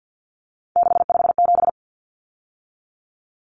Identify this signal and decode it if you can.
Morse code, heard as a keyed tone, about 50 wpm, 700 Hz. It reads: 657
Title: Numbers - 3-Digit 50wpm